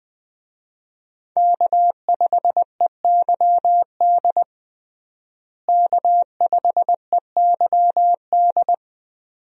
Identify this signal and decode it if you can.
Morse code, heard as a keyed tone, about 21 wpm, 700 Hz. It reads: K5EYD K5EYD
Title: USA Call Signs - 5 Characters - Repeated 2x - 20wpm